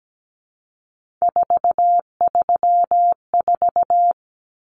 Morse code, heard as a keyed tone, about 17 wpm, 700 Hz. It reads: 434
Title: Common Cut Numbers - 3-Digit 17wpm